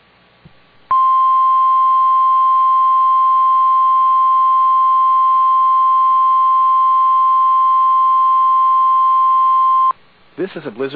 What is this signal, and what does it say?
Emergency Alert System (SAME) tones (the 1050 Hz attention tone), with speech. It is heard from 0.9 to 10.0 s.